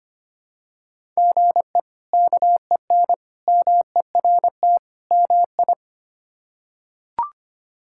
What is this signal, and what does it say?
Morse code, heard as a keyed tone, about 25 wpm, 700 Hz. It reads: GE KEN MERT MI
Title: K1USN SST QSO Practice - 25wpm